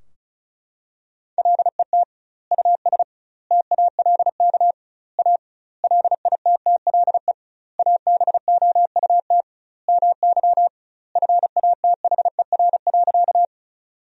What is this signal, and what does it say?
Morse code, heard as a keyed tone, about 34 wpm, 700 Hz. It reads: LET US TALK A LITTLE ABOUT MY FATHER.
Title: 2x Word-Spacing - Sentences from Top 400 Words - 35wpm